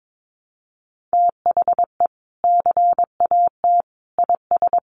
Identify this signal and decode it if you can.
Morse code, heard as a keyed tone, about 22 wpm, 700 Hz. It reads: THE CAT IS
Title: Additional Easy Phrases 22wpm